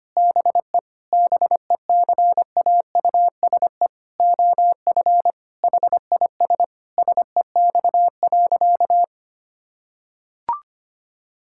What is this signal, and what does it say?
Morse code, heard as a keyed tone, about 25 wpm, 700 Hz. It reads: BE BECAUSE OF HIS SEX.
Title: Sentences from Top 1600 Words 25wpm